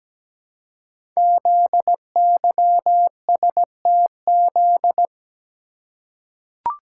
Morse code, heard as a keyed tone, about 17 wpm, 700 Hz. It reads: ZYSTZ
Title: Code Groups - 5 Letters - 17wpm